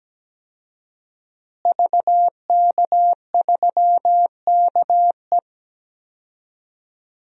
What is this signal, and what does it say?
Morse code, heard as a keyed tone, about 17 wpm, 700 Hz. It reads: VK3KE
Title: DXCC-100 Call Signs - 5 Characters - Repeated 3x - 17wpm